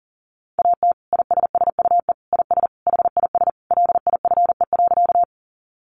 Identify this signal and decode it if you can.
Morse code, heard as a keyed tone, about 37 wpm, 700 Hz. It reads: AT ISSUE IS HIS LIFE.
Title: Sentences from Top 900 Words 40wpm